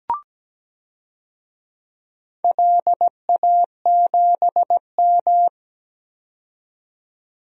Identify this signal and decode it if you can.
Morse code, heard as a keyed tone, about 17 wpm, 700 Hz. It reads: LA7M
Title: DXCC-100 Call Signs - 4 Characters - 17wpm